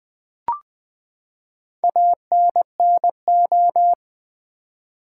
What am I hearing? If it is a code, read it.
Morse code, heard as a keyed tone, about 20 wpm, 700 Hz. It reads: ANNO
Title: Build A Word - 4 Letters At A Time - 12-Letter Words - 20wpm